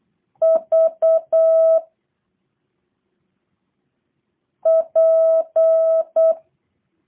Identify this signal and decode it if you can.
Morse code, heard as a keyed tone, about 8 wpm, 650 Hz. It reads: V P
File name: jh8cbh / 2023cw / review1-5